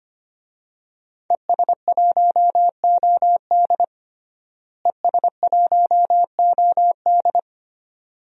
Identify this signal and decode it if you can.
Morse code, heard as a keyed tone, about 25 wpm, 700 Hz. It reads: ES1OD ES1OD
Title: DXCC-100 Call Signs - 5 Characters - Repeated 2x - 25wpm